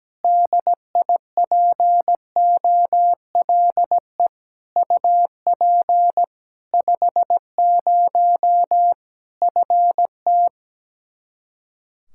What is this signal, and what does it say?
Morse code, heard as a keyed tone, about 17 wpm, 700 Hz. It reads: DIPOLE UP 50 FT
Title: K1USN SST QSO Practice 17wpm